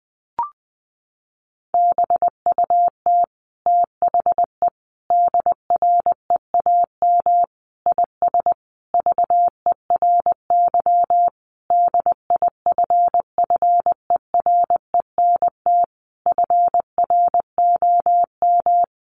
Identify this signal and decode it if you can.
Morse code, heard as a keyed tone, about 20 wpm, 700 Hz. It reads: BUT THE DREAM IS VERY DIFFERENT FROM
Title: Sentences from Top 1500 Words 20wpm